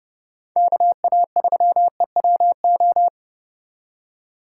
Morse code, heard as a keyed tone, about 30 wpm, 700 Hz. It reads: KA3EWO